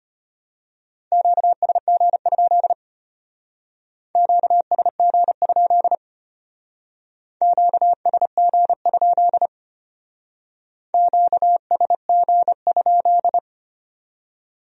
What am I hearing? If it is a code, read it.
Morse code, heard as a keyed tone, about 31 wpm, 700 Hz. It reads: QSG? QSG? QSG? QSG?